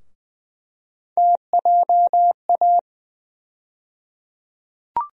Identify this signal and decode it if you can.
Morse code, heard as a keyed tone, about 20 wpm, 700 Hz. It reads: TJA
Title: USA Call Sign Postfixes - 3 Letters 20wpm